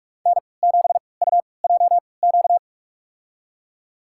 Morse code, heard as a keyed tone, about 42 wpm, spaced as for 25 wpm, 700 Hz. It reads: N7UJQ
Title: USA Call Signs - 5 Characters - 3x Character Spacing - 45wpm